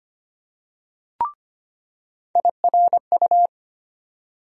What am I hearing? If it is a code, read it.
Morse code, heard as a keyed tone, about 25 wpm, 700 Hz. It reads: IRU
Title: USA Call Sign Postfixes - 3 Letters 25wpm